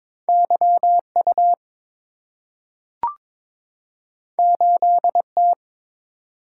Morse code, heard as a keyed tone, about 22 wpm, 700 Hz. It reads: YU 8T
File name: practice / 22wpm / Two Letter-Number Combinations - 22wpm